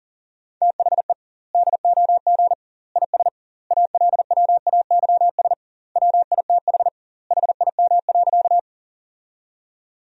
Morse code, heard as a keyed tone, about 40 wpm, 700 Hz. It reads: THE DOG IS ALWAYS WITH HIM.
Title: 2x Word-Spacing - Sentences from Top 400 Words - 40wpm